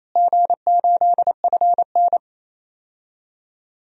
Morse code, heard as a keyed tone, about 28 wpm, 700 Hz. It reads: G8FN
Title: DXCC-100 Call Signs - 4 Characters - Repeated 3x - 28wpm